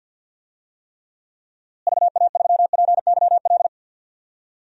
Morse code, heard as a keyed tone, about 51 wpm, 700 Hz. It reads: VA3PYL
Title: DXCC-100 Call Signs - 6 Characters - Repeated 3x - 50wpm